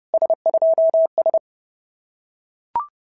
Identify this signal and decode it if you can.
Morse code, heard as a keyed tone, about 30 wpm, 690 Hz. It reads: S2S
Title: SOTA QSO Practice 30wpm